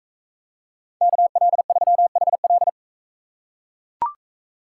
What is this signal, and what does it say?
Morse code, heard as a keyed tone, about 39 wpm, 700 Hz. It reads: KL3HL